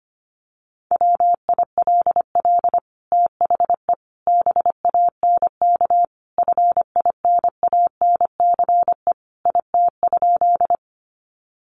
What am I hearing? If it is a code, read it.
Morse code, heard as a keyed tone, about 25 wpm, 700 Hz. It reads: WILL THE BANK FINANCE IT?
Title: Sentences from Top 1500 Words 25wpm